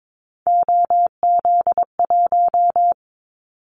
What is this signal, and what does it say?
Morse code, heard as a keyed tone, about 22 wpm, 700 Hz. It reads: OZ1
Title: Callsign Prefixes for Top 100 DXCC - 22wpm